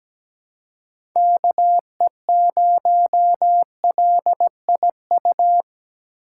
Morse code, heard as a keyed tone, about 17 wpm, 700 Hz. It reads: KE0LIU
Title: USA Call Signs - 6 Characters - 4x Character Spacing - 17wpm